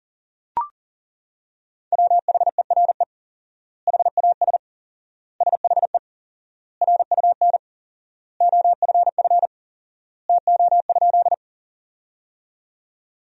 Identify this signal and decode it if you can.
Morse code, heard as a keyed tone, about 40 wpm, 700 Hz. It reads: WHERE HAS SHE RUN OFF TO?
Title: 4x Word-Spacing - Sentences from Top 300 Words - 40wpm